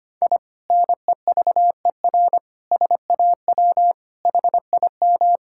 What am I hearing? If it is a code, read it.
Morse code, heard as a keyed tone, about 25 wpm, 700 Hz. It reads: I NEVER SAW HIM
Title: Sentences from Top 1000 Words - Review Part 6 - 25wpm